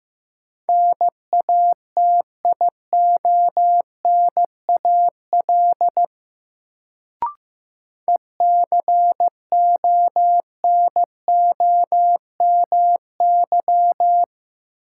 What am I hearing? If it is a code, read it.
Morse code, heard as a keyed tone, about 15 wpm, 700 Hz. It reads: NATIONAL ECONOMY
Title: Sentences from Top 1200 Words - New Word Drill Alt - 15wpm